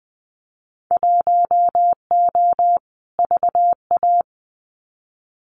Morse code, heard as a keyed tone, about 20 wpm, 700 Hz. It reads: 1O VA
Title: Winter Field Day Exchanges 20wpm